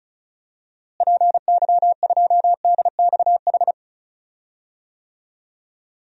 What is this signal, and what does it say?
Morse code, heard as a keyed tone, about 34 wpm, 700 Hz. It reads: PY2DXH
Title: DXCC-100 Call Signs - 6 Characters - Repeated 3x - 35wpm